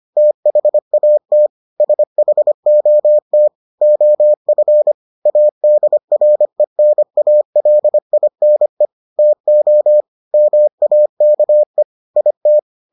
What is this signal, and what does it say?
Morse code, heard as a keyed tone, about 25 wpm, 600 Hz. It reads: THAT SHOT OF ADRENALINE TO MAKE IT